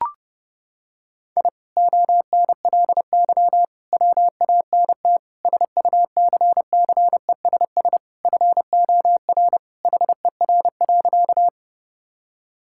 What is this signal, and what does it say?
Morse code, heard as a keyed tone, about 32 wpm, 700 Hz. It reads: I ONLY WANT SUCCESS FOR HER.
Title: Sentences from Top 1000 Words - Review Part 6 - 30wpm